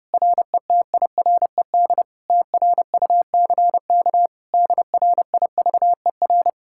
Morse code, heard as a keyed tone, about 33 wpm, 700 Hz. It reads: RETIRED TRUCK DRIVER